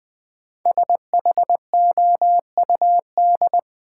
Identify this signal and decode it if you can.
Morse code, heard as a keyed tone, about 20 wpm, 700 Hz. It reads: SHOUD
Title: Sentences from Top 1000 Words - Review Part 4 - New Word Drill - 20wpm